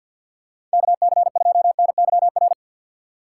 Morse code, heard as a keyed tone, about 50 wpm, 690 Hz. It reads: KC2NYR
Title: USA Call Signs - 2x Character Spacing - 50wpm